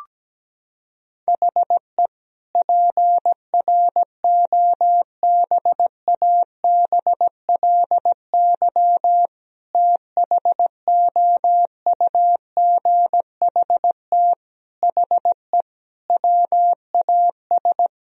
Morse code, heard as a keyed tone, about 17 wpm, 700 Hz. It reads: HE PROBABLY THOUGHT HE WAS
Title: Sentences from Top 1000 Words - Review Part 4 - 17wpm